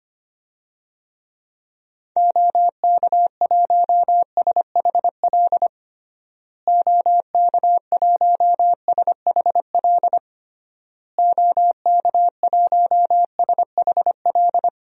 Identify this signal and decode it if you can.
Morse code, heard as a keyed tone, about 25 wpm, 700 Hz. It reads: OK1SHL OK1SHL OK1SHL
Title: DXCC-100 Call Signs - 6 Characters - Repeated 3x - 25wpm